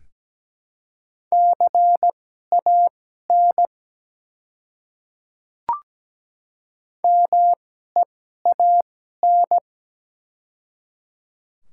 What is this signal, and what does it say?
Morse code, heard as a keyed tone, about 17 wpm, spaced as for 12 wpm, 700 Hz. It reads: CAN MEAN
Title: 2x Character Spacing - Top 200 Words - 17wpm